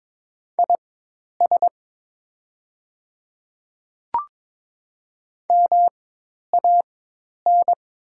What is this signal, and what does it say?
Morse code, heard as a keyed tone, about 22 wpm, spaced as for 10 wpm, 700 Hz. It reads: IS MAN
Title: 4x Character Spacing - Top 200 Words - 22wpm